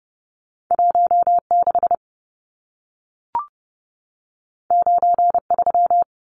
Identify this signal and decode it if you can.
Morse code, heard as a keyed tone, about 30 wpm, 700 Hz. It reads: 16 93
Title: Numbers - 2-Digit 30wpm